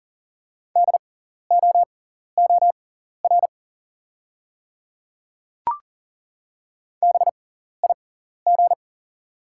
Morse code, heard as a keyed tone, about 40 wpm, spaced as for 14 wpm, 700 Hz. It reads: DOOR BIG
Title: 6x Character Spacing - Top 300 Words - 40wpm